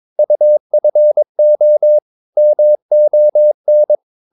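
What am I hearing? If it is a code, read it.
Morse code, heard as a keyed tone, about 23 wpm, 600 Hz. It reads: UFO MON
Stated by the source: Morse practice of short Danish words at 22 wpm, from CW med Gnister.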